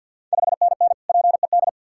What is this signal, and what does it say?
Morse code, heard as a keyed tone, about 46 wpm, 690 Hz. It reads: 5NN PED
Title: Ontario QSO Party Exchange 50wpm